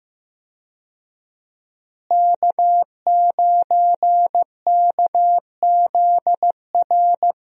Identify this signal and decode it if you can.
Morse code, heard as a keyed tone, about 15 wpm, 700 Hz. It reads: K9KZR
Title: USA Call Signs - 5 Characters - Repeated 2x - 15wpm